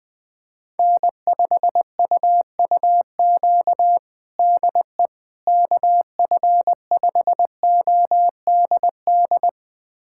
Morse code, heard as a keyed tone, about 20 wpm, 700 Hz. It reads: N5UUQ DE KF5ODD
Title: USA Callsign DE Callsign 20wpm